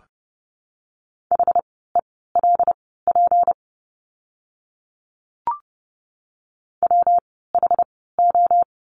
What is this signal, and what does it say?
Morse code, heard as a keyed tone, about 30 wpm, spaced as for 17 wpm, 700 Hz. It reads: HELP WHO
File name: course / 30:10wpm / 237 Lesson 38f - Letter q - Words 30wpm